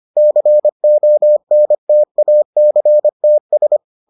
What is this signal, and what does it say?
Morse code, heard as a keyed tone, about 25 wpm, 600 Hz. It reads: CONTACTS